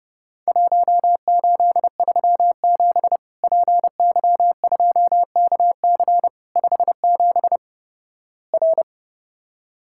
Morse code, heard as a keyed tone, about 30 wpm, 700 Hz. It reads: 1837 PY2KC 57 R
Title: Worked All Europe DX Contest - QTC Exchange - 30wpm